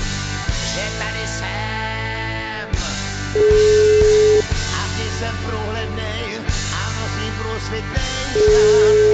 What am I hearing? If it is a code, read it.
Telephone call progress tone (Europe, 425 Hz), heard as a telephone network's tone, about 420 Hz, with noise. Ringback tone.